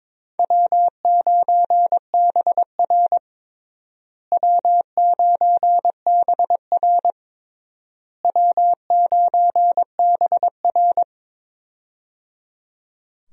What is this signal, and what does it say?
Morse code, heard as a keyed tone, about 22 wpm, 700 Hz. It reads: W9BR W9BR W9BR
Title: USA Call Signs - 4 Characters - Repeated 3x - 22wpm